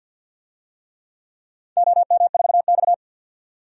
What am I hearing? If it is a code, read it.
Morse code, heard as a keyed tone, about 50 wpm, 700 Hz. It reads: OM4X
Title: DXCC-100 Call Signs - 4 Characters - 50wpm